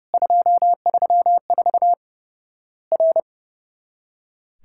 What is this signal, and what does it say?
Morse code, heard as a keyed tone, about 30 wpm, 700 Hz. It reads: 234 R
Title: Worked All Europe DX Contest - QTC Exchange - 30wpm